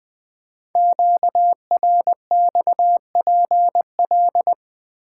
Morse code, heard as a keyed tone, about 20 wpm, 700 Hz. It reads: QRXPL
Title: Code Groups - 5 Letters - 20wpm